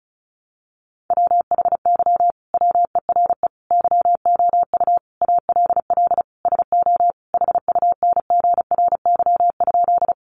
Morse code, heard as a keyed tone, about 34 wpm, 700 Hz. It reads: WHY WERE YOU ALL SO HUNGRY?
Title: Sentences from Top 1400 Words 35wpm